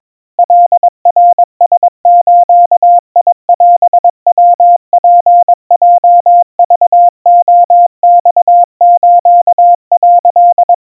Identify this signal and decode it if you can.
Morse code (Japanese Wabun code), heard as a keyed tone, about 22 wpm, 700 Hz. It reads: カナラズオヤツヲクレマス」
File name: kamyu 80 cpm 700 Hz Crispy